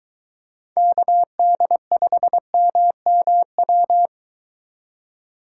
Morse code, heard as a keyed tone, about 22 wpm, 700 Hz. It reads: KD5MMW